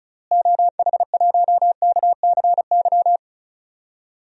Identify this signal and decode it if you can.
Morse code, heard as a keyed tone, about 36 wpm, 700 Hz. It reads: OH1KCY
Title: DXCC-100 Call Signs - 6 Characters - Repeated 3x - 35wpm